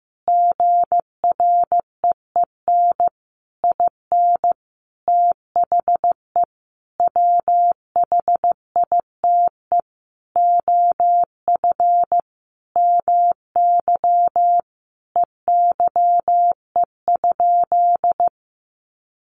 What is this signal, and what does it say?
Morse code, heard as a keyed tone, about 15 wpm, 700 Hz. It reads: GREEN IN THE WHITE OF MY EYE?
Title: Sentences from Top 500 Words 15wpm